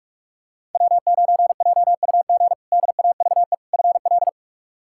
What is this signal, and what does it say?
Morse code, heard as a keyed tone, about 46 wpm, 700 Hz. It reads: W9JUG DAVE FL
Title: K1USN SST QSO Practice - 45wpm